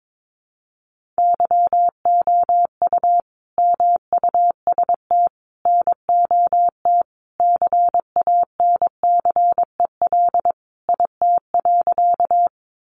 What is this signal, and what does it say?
Morse code, heard as a keyed tone, about 22 wpm, 700 Hz. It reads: YOU MUST NOT CANCEL IT.
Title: Sentences from Top 1200 Words 22wpm